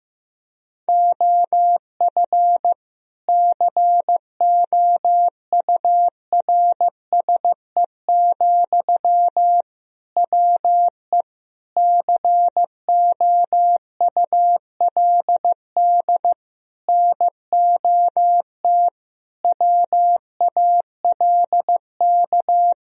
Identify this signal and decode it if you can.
Morse code, heard as a keyed tone, about 16 wpm, 700 Hz. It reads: OF COURSE, WE COULD NOT WALK
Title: Sentences from Top 1000 Words - Review Part 2 - 15wpm